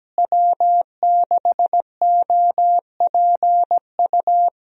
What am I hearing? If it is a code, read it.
Morse code, heard as a keyed tone, about 17 wpm, 700 Hz. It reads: W6OPU
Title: USA Call Signs - 5 Characters - Repeated 2x - 17wpm